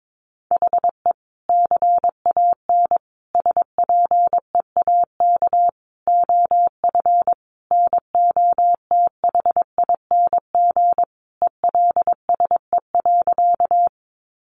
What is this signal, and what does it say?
Morse code, heard as a keyed tone, about 22 wpm, 700 Hz. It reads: HE CAN SPEAK OF NOTHING ELSE.
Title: Sentences from Top 1100 Words 22wpm